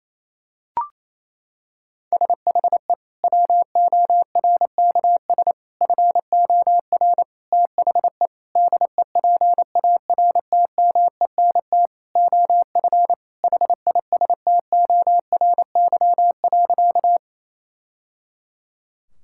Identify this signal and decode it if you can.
Morse code, heard as a keyed tone, about 28 wpm, 700 Hz. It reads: SHE WORKS FOR THE DEPARTMENT OF HISTORY.
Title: Sentences from Top 1000 Words - Review Part 5 - 28wpm